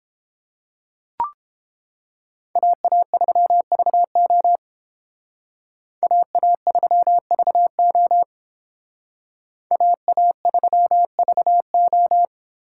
Morse code, heard as a keyed tone, about 30 wpm, 700 Hz. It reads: AA3VO AA3VO AA3VO